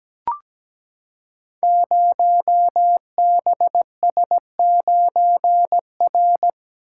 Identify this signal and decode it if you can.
Morse code, heard as a keyed tone, about 17 wpm, 700 Hz. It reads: 0BS9R
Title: Code Groups - 5 Letter:Numbers - 17wpm